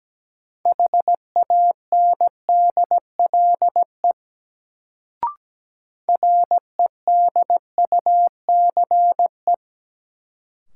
Morse code, heard as a keyed tone, about 17 wpm, 700 Hz. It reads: HANDLE REDUCE
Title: Sentences from Top 1100 Words - New Word Drill - 17wpm